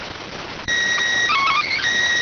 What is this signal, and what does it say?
SSTV (slow-scan television), PD 290.